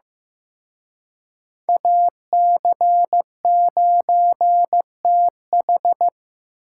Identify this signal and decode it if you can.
Morse code, heard as a keyed tone, about 18 wpm, 700 Hz. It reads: AC9TH
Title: Speed Racing - USA 2x2 Call Signs 15wpm